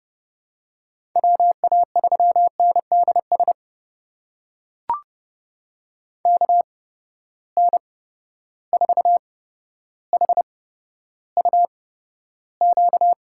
Morse code, heard as a keyed tone, about 30 wpm, spaced as for 13 wpm, 700 Hz. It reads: WA3NDS KN4HUQ
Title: USA Call Signs - 6 Characters - 8x Character Spacing - 30wpm